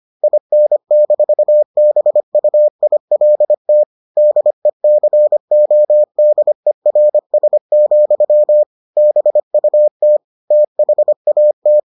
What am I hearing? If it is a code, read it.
Morse code, heard as a keyed tone, about 25 wpm, 600 Hz. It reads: IN-BUILT DECODERS, BUT THAT